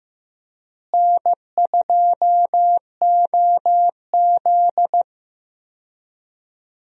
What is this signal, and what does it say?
Morse code, heard as a keyed tone, about 15 wpm, 700 Hz. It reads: N2OZ
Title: USA Call Signs - 4 Characters - Repeated 2x - 15wpm